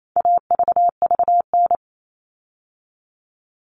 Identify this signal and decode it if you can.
Morse code, heard as a keyed tone, about 28 wpm, 700 Hz. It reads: AVVN
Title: All Cut Numbers - 4-Digit 28wpm